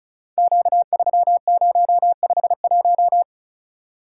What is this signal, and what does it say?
Morse code, heard as a keyed tone, about 36 wpm, 700 Hz. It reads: Q3051
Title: Code Groups - 5 Letter:Numbers - 35wpm